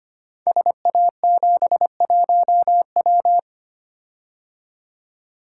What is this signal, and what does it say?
Morse code, heard as a keyed tone, about 25 wpm, 700 Hz. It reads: SA71W
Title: Code Groups - 5 Letter:Numbers - 25wpm